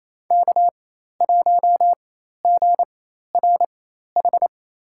Morse code, heard as a keyed tone, about 28 wpm, spaced as for 13 wpm, 700 Hz. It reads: K1GRH
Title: USA Call Signs - 5 Characters - 4x Character Spacing - 28wpm